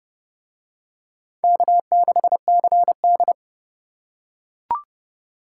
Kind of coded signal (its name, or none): Morse code